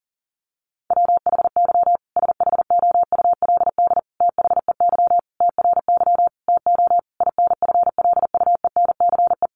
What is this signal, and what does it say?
Morse code, heard as a keyed tone, about 40 wpm, 700 Hz. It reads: WHY SHOULD THEY TRY TO INFLUENCE
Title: Sentences from Top 1100 Words 40wpm